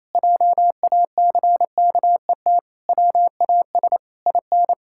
Morse code, heard as a keyed tone, about 28 wpm, 700 Hz. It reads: JACKET WAS IN
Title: Pangram Challenge v2 - 28wpm